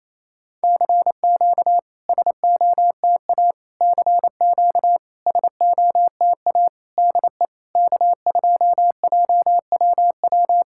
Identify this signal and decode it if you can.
Morse code, heard as a keyed tone, about 28 wpm, 700 Hz. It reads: CQ SOTA CQ SOTA DE K2JWW
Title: SOTA QSO Practice 28wpm